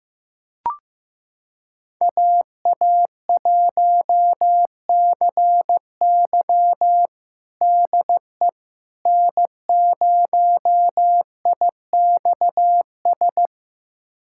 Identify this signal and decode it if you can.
Morse code, heard as a keyed tone, about 15 wpm, 700 Hz. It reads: AA1CY DE N0IXS
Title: USA Callsign DE Callsign - 5 Characters - 15wpm